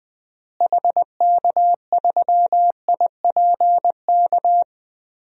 Morse code, heard as a keyed tone, about 21 wpm, 700 Hz. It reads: HK3IPK
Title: DXCC-100 Call Signs - 6 Characters - Repeated 3x - 20wpm